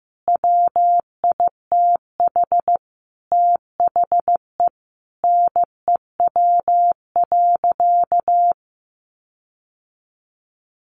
Morse code, heard as a keyed tone, about 15 wpm, 700 Hz. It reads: WITH THE NEW.
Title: Sentences from Top 1300 Words 15wpm